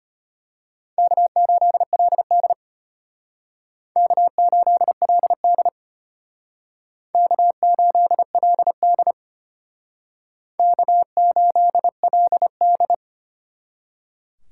Morse code, heard as a keyed tone, about 31 wpm, 700 Hz. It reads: K8LD K8LD K8LD K8LD